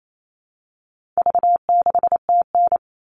Morse code, heard as a keyed tone, about 28 wpm, 700 Hz. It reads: V6TN
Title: All Cut Numbers - 4-Digit 28wpm